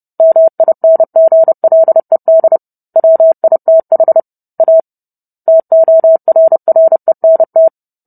Morse code, heard as a keyed tone, about 30 wpm, 650 Hz. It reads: MINGLED WITH A TORRENT